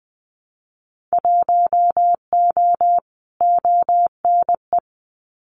Morse code, heard as a keyed tone, about 20 wpm, 700 Hz. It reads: 1O ONE
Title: Winter Field Day Exchanges 20wpm